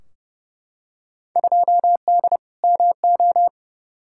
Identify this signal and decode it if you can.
Morse code, heard as a keyed tone, about 30 wpm, 700 Hz. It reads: 2D MO
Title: Field Day Exchanges - 30wpm